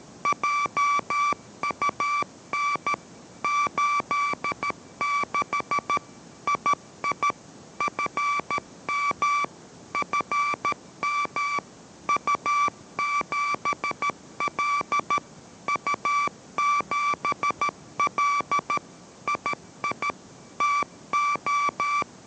Morse code, heard as a keyed tone, about 13 wpm, 1160 Hz. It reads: JUN 86 II FM FM U7L U7L II TO